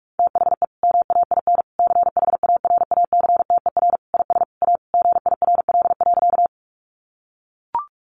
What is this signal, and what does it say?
Morse code, heard as a keyed tone, about 44 wpm, 700 Hz. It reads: THE MAIN CHARACTER IS A GIRL.
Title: Additional Easy Phrases 45wpm